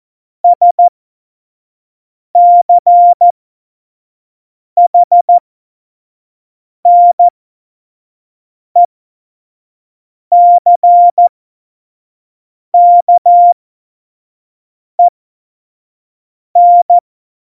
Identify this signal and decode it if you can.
Morse code, heard as a keyed tone, about 14 wpm, spaced as for 5 wpm, 700 Hz. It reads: SCHNECKEN
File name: Morsekurs / Witze / Witz 13. November 2018, 25 F70